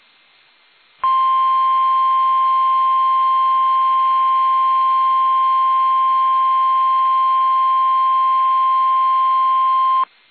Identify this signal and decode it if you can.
Emergency Alert System (SAME) tones (the 1050 Hz attention tone). It is heard from 1.0 to 10.1 s.